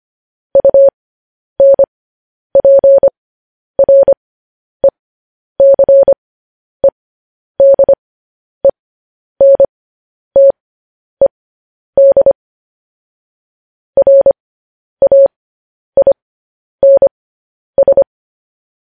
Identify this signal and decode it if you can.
Morse code, heard as a keyed tone, about 25 wpm, spaced as for 10 wpm, 550 Hz. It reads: UNPRECEDENTED RAINS
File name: podcast / News headlines-1x-25x10